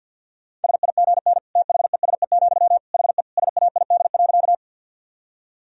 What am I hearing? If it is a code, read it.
Morse code, heard as a keyed tone, about 50 wpm, 700 Hz. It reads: SIGN THESE, HE SAID.
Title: Sentences from Top 1000 Words - Review Part 1 - 50wpm